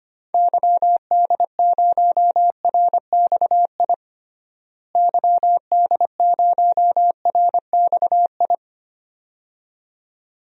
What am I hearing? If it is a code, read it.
Morse code, heard as a keyed tone, about 25 wpm, 700 Hz. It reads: YD0RXI YD0RXI
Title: DXCC-100 Call Signs - 6 Characters - Repeated 2x - 25wpm